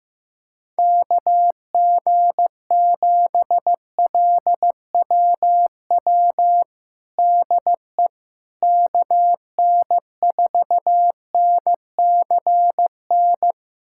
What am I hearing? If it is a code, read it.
Morse code, heard as a keyed tone, about 15 wpm, 700 Hz. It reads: KG7LWW DE KN4NCN
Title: USA Callsign DE Callsign 15wpm